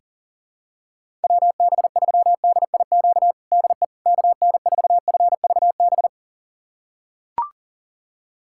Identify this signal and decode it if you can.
Morse code, heard as a keyed tone, about 40 wpm, 700 Hz. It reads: WB3DIQ DE KN4FVB
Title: USA Callsign DE Callsign - 6 Characters - 40wpm